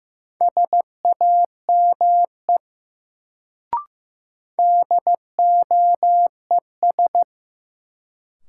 Morse code, heard as a keyed tone, about 15 wpm, 700 Hz. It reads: SAME DOES
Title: Sentences from Top 200 Words - New Word Drill - 15wpm